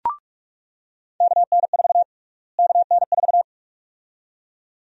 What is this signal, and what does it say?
Morse code, heard as a keyed tone, about 46 wpm, 700 Hz. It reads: KN4 KN4
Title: USA Call Sign Prefixes - Repeated 2x - 45wpm